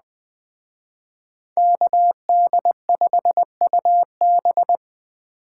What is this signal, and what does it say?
Morse code, heard as a keyed tone, about 20 wpm, 700 Hz. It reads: KD5UB
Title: Speed Racing - USA 2x2 Call Signs 17wpm